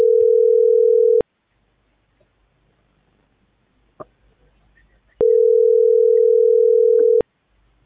Ringback tone (North America), 440 + 480 Hz.